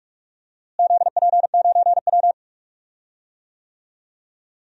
Morse code, heard as a keyed tone, about 41 wpm, 700 Hz. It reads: ZP9W
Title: DXCC-100 Call Signs - 4 Characters - Repeated 3x - 45wpm